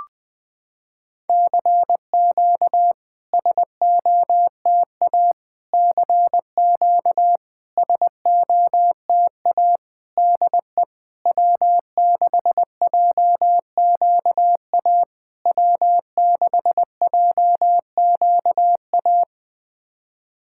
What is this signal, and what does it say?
Morse code, heard as a keyed tone, about 20 wpm, 700 Hz. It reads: CQ SOTA CQ SOTA DE W6JQA W6JQA
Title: SOTA QSO Practice 20wpm